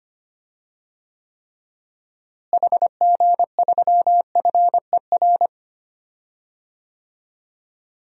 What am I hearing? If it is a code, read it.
Morse code, heard as a keyed tone, about 25 wpm, 700 Hz. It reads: HG3FER
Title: DXCC-100 Call Signs - 6 Characters - Repeated 2x - 25wpm